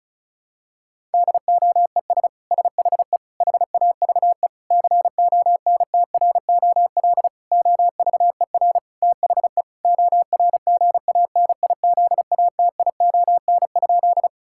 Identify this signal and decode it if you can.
Morse code, heard as a keyed tone, about 34 wpm, 700 Hz. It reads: DOES SHE HAVE CONTROL OVER THE ORGANIZATION?
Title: Sentences from Top 1000 Words - Review Part 5 - 35wpm